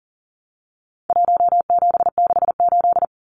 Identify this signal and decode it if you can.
Morse code, heard as a keyed tone, about 40 wpm, 700 Hz. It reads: 1768
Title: Numbers - Contest Style - 4-Digit 40wpm